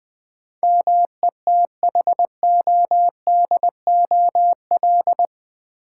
Morse code, heard as a keyed tone, about 20 wpm, 700 Hz. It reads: METHODOL
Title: Build A Word - 4 Letters At A Time - 11-Letter Words - 20wpm